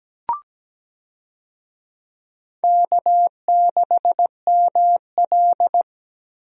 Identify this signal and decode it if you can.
Morse code, heard as a keyed tone, about 17 wpm, 700 Hz. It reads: K6ML
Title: USA Call Signs - 4 Characters - Repeated 2x - 17wpm